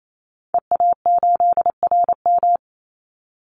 Morse code, heard as a keyed tone, about 28 wpm, 700 Hz. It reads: EA8RM
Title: Call Signs - 28wpm